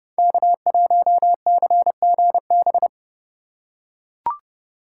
Morse code, heard as a keyed tone, about 30 wpm, 700 Hz. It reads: K1CGB
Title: Code Groups - 5 Letter:Numbers - 30wpm